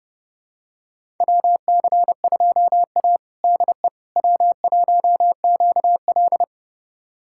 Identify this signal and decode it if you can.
Morse code, heard as a keyed tone, about 32 wpm, 700 Hz. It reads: WC2A DE W1QL
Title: USA Callsign DE Callsign - 4 Characters - 30wpm